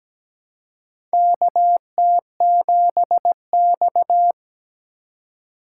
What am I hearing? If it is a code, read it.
Morse code, heard as a keyed tone, about 17 wpm, 700 Hz. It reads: KT7X
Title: USA Call Signs - 4 Characters - 3x Character Spacing - 17wpm